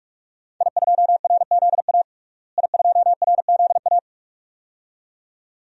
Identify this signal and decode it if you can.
Morse code, heard as a keyed tone, about 45 wpm, 700 Hz. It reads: I2RZA I2RZA
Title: DXCC-100 Call Signs - 5 Characters - Repeated 2x - 45wpm